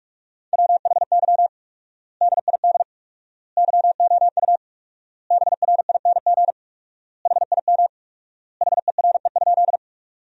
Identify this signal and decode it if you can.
Morse code, heard as a keyed tone, about 41 wpm, 700 Hz. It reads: WHY DID YOU BRING HIM HERE?
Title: 4x Word-Spacing - Sentences from Top 500 Words - 45wpm